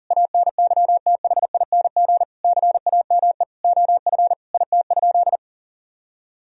Morse code, heard as a keyed tone, about 40 wpm, 700 Hz. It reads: ANYTHING CAME OF IT?
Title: Sentences from Top 1000 Words - Review Part 3 - 40wpm